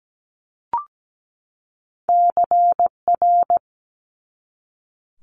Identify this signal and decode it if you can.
Morse code, heard as a keyed tone, about 17 wpm, 700 Hz. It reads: CR
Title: Multi-letter Phonemes 17wpm